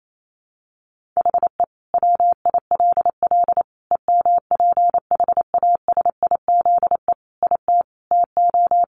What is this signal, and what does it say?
Morse code, heard as a keyed tone, about 28 wpm, 700 Hz. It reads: HE WILL EMPHASIZE IT TO
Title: Sentences from Top 1500 Words 28wpm